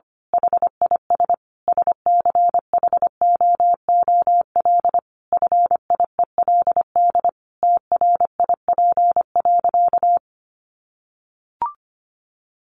Morse code, heard as a keyed tone, about 25 wpm, 700 Hz. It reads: HIS SCHOOL FIELD TRIP.
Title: Additional Easy Phrases 25wpm